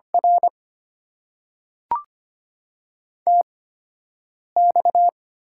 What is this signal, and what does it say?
Morse code, heard as a keyed tone, about 25 wpm, spaced as for 7 wpm, 700 Hz. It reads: R TX
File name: Top 90 QSO Elements - 8x Character Spacing - 25wpm